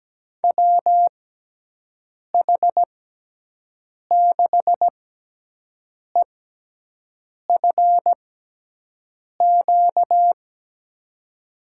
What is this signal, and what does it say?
Morse code, heard as a keyed tone, about 17 wpm, spaced as for 6 wpm, 700 Hz. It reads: WH6EFQ
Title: USA Call Signs - 6 Characters - 6x Character Spacing - 17wpm